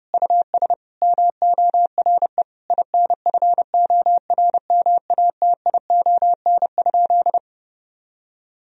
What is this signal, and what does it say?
Morse code, heard as a keyed tone, about 32 wpm, 700 Hz. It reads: US MORE INFORMATION?
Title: Sentences from Top 1000 Words - Review Part 3 - 30wpm